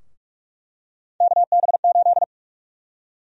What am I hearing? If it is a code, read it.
Morse code, heard as a keyed tone, about 42 wpm, 700 Hz. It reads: KB8
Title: USA Call Sign Prefixes - Repeated 2x - 45wpm